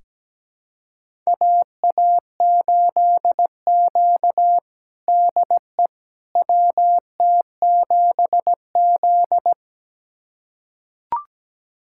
Morse code, heard as a keyed tone, about 17 wpm, 700 Hz. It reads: AA8Q DE WT7Z
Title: USA Callsign DE Callsign - 4 Characters - 17wpm